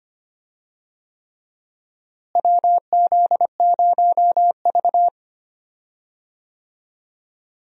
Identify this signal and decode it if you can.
Morse code, heard as a keyed tone, about 25 wpm, 700 Hz. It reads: WZ0V
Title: USA Call Signs - 4 Characters - Repeated 3x - 25wpm